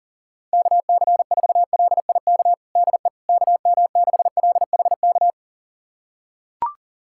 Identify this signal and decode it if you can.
Morse code, heard as a keyed tone, about 40 wpm, 690 Hz. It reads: KC4LIK DE KM6LHK
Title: USA Callsign DE Callsign - 6 Characters - 40wpm